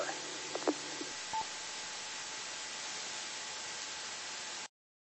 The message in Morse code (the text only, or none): none